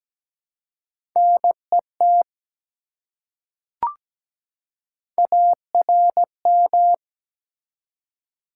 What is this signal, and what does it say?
Morse code, heard as a keyed tone, about 17 wpm, 700 Hz. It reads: NET ARM
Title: Three Letter Words - 17wpm